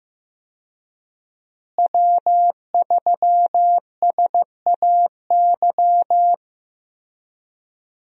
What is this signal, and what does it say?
Morse code, heard as a keyed tone, about 15 wpm, 700 Hz. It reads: W3SAY
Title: USA Call Signs - 5 Characters - Repeated 2x - 15wpm